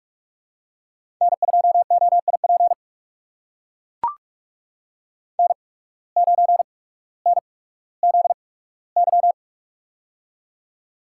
Morse code, heard as a keyed tone, about 44 wpm, spaced as for 19 wpm, 700 Hz. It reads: N2OIP N9NZY
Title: USA Call Signs - 5 Characters - 8x Character Spacing - 45wpm